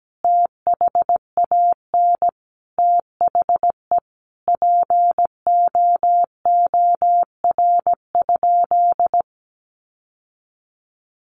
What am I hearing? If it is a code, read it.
Morse code, heard as a keyed tone, about 17 wpm, 700 Hz. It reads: THAN THE POOR?
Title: Sentences from Top 1100 Words 17wpm